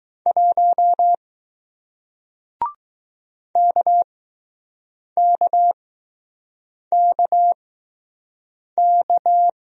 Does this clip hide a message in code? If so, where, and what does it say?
Yes: Morse code from 0.3 to 9.5 s, heard as a keyed tone, about 20 wpm, spaced as for 6 wpm, 700 Hz. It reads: 1 KKKK